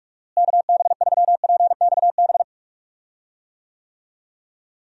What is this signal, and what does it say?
Morse code, heard as a keyed tone, about 43 wpm, 700 Hz. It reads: KB3PXB
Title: USA Call Signs - 6 Characters - Repeated 2x - 45wpm